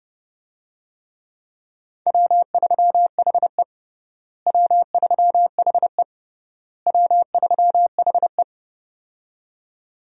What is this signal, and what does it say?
Morse code, heard as a keyed tone, about 30 wpm, 700 Hz. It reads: W3HE W3HE W3HE
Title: USA Call Signs - 4 Characters - Repeated 3x - 30wpm